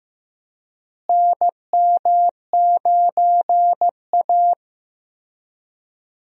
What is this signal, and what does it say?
Morse code, heard as a keyed tone, about 15 wpm, 700 Hz. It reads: NM9A
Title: USA Call Signs - 4 Characters - Repeated 3x - 15wpm